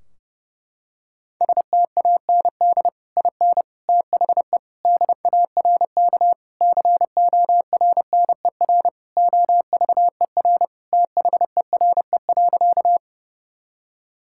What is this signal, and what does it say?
Morse code, heard as a keyed tone, about 30 wpm, 700 Hz. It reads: STAND IN THE DARK CORNER OVER THERE.
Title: Sentences from Top 1600 Words 30wpm